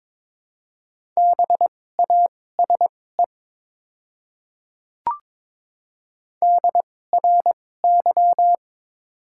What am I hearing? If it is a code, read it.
Morse code, heard as a keyed tone, about 21 wpm, spaced as for 16 wpm, 700 Hz. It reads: BASE DRY
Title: 2x Character Spacing - Top 500 Words - 22wpm